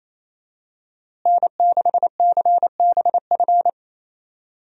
Morse code, heard as a keyed tone, about 28 wpm, 700 Hz. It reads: N6CBF
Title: USA Call Signs - 5 Characters - Repeated 2x - 28wpm